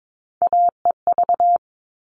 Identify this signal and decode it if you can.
Morse code, heard as a keyed tone, about 22 wpm, 700 Hz. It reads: AEV
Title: All Cut Numbers - 3-Digit 22wpm